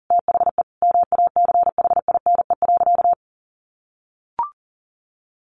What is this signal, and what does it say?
Morse code, heard as a keyed tone, about 40 wpm, 700 Hz. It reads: THE MACHINE.
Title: A Sound Of Thunder 40wpm